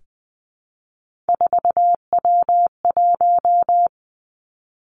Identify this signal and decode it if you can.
Morse code, heard as a keyed tone, about 20 wpm, 700 Hz. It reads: 4W1
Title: Three Challenging Letter-Number Combinations 20wpm